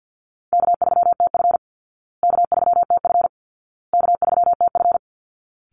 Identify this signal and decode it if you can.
Morse code, heard as a keyed tone, about 50 wpm, 700 Hz. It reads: K3TF K3TF K3TF
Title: Easy Call Signs Repeated 3x - 50wpm